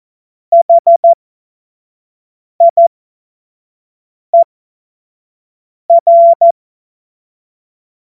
Morse code, heard as a keyed tone, about 14 wpm, spaced as for 5 wpm, 680 Hz. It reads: HIER
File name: Morsekurs / Witze / Witz 01. Juni 2021, 25 F70